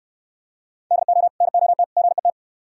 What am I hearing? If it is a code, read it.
Morse code, heard as a keyed tone, about 74 wpm, 700 Hz. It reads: DO NOT GET